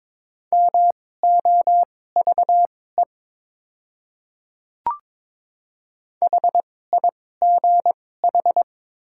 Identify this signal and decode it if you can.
Morse code, heard as a keyed tone, about 22 wpm, spaced as for 16 wpm, 700 Hz. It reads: MOVE HIGH
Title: 2x Character Spacing - Top 200 Words - 22wpm